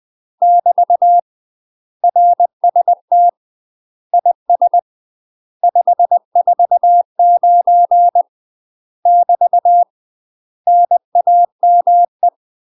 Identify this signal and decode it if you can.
Morse code, heard as a keyed tone, about 20 wpm, 700 Hz. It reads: <BT> RST IS 549 <BT> NAME
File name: UK MASTER QSOs TRAINING / 20WPM / UK MASTER QSOs 0062